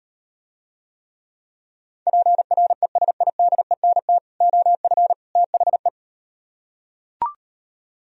Morse code, heard as a keyed tone, about 38 wpm, 700 Hz. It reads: PRESIDENT OF THE